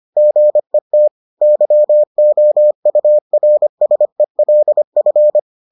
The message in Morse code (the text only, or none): GET YOURSELF